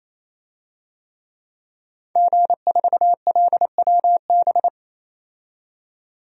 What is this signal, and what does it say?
Morse code, heard as a keyed tone, about 28 wpm, 700 Hz. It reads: G4LWB
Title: DXCC-100 Call Signs - 5 Characters - 28wpm